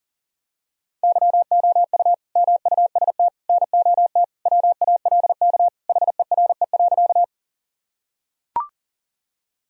Morse code, heard as a keyed tone, about 40 wpm, 700 Hz. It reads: YOU MUST NOT WALK HERE.
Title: Sentences from Top 1000 Words - Review Part 1 - 40wpm